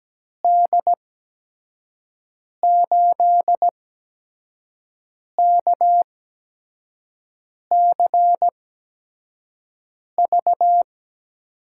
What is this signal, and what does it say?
Morse code, heard as a keyed tone, about 17 wpm, spaced as for 5 wpm, 700 Hz. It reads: D8KCV
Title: USA Call Signs - 6 Characters - 8x Character Spacing - 17wpm